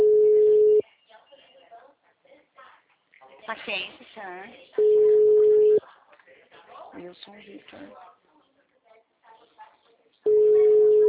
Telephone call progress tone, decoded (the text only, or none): ringback tone